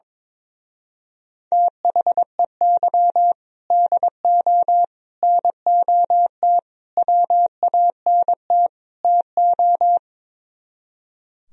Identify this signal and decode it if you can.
Morse code, heard as a keyed tone, about 22 wpm, 700 Hz. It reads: THEY DO NOT WANT TO
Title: Speed-Racing - Sets of 5 Words 22wpm